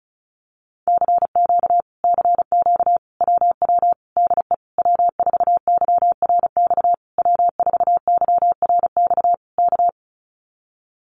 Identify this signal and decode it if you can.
Morse code, heard as a keyed tone, about 34 wpm, 700 Hz. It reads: CQ CQ WW DE W4YRX W4YRX K
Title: Common QSO Phrases Encore 35wpm